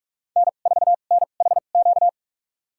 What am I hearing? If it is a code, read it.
Morse code, heard as a keyed tone, about 43 wpm, spaced as for 32 wpm, 700 Hz. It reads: N4NHQ
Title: USA Call Signs - 5 Characters - 2x Character Spacing - 45wpm